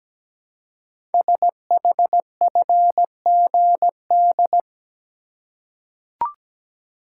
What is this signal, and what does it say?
Morse code, heard as a keyed tone, about 17 wpm, 700 Hz. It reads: SHFGD
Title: Code Groups - 5 Letters - 17wpm